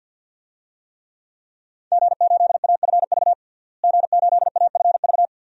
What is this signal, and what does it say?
Morse code, heard as a keyed tone, about 51 wpm, 700 Hz. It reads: G8AFV G8AFV
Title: DXCC-100 Call Signs - 5 Characters - Repeated 2x - 50wpm